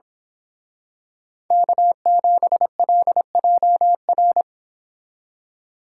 Morse code, heard as a keyed tone, about 26 wpm, 700 Hz. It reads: K7LJR
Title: Speed Racing - USA 1x3 Call Signs 22wpm